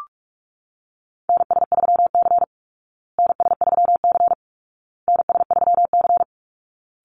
Morse code, heard as a keyed tone, about 44 wpm, 700 Hz. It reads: NS3C NS3C NS3C
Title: Call Signs Repeated 2x - 45wpm